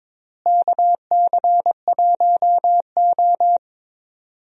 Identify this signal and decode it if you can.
Morse code, heard as a keyed tone, about 22 wpm, 700 Hz. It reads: KC1O
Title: USA Call Signs - 4 Characters - Repeated 2x - 22wpm